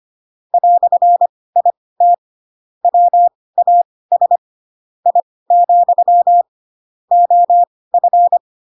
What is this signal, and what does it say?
Morse code, heard as a keyed tone, about 25 wpm, spaced as for 18 wpm, 700 Hz. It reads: "IT WAS I, OF
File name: Pinocchio / Pinocchio ch2 12 25 ik1hge